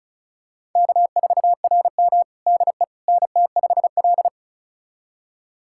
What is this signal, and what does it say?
Morse code, heard as a keyed tone, about 36 wpm, 700 Hz. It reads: K4RM DE NT5L
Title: USA Callsign DE Callsign - 4 Characters - 35wpm